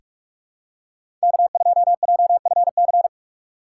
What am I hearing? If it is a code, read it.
Morse code, heard as a keyed tone, about 44 wpm, 700 Hz. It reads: K2JFC
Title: USA Call Signs - 5 Characters - 3x Character Spacing - 45wpm